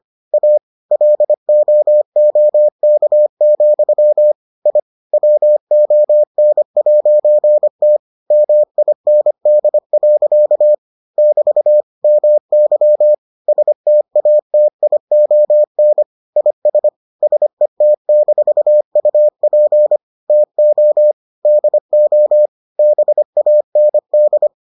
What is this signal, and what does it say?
Morse code, heard as a keyed tone, about 25 wpm, 600 Hz. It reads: A LOOK, I WON'T MIND. <BT> MY STATION IS SET-UP TO DO BAND